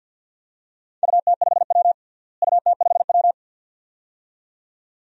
Morse code, heard as a keyed tone, about 51 wpm, 700 Hz. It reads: UT5W UT5W
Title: DXCC-100 Call Signs - 4 Characters - Repeated 2x - 50wpm